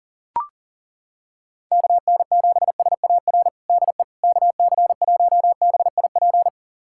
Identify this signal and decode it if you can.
Morse code, heard as a keyed tone, about 37 wpm, 700 Hz. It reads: KN7SAR DE KC1BIP
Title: USA Callsign DE Callsign 40wpm